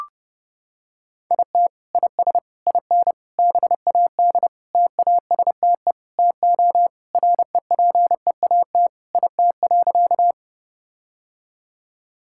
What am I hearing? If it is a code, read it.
Morse code, heard as a keyed tone, about 30 wpm, 700 Hz. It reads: IT IS IN BAD TASTE TO REPEAT IT.
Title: Sentences from Top 1600 Words 30wpm